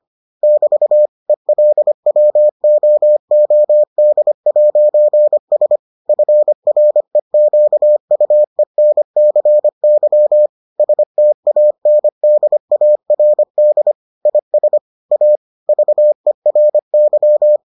Morse code, heard as a keyed tone, about 25 wpm, 600 Hz. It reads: <BT> ELWOOD'S FREQUENCY STANDARD IS A VERY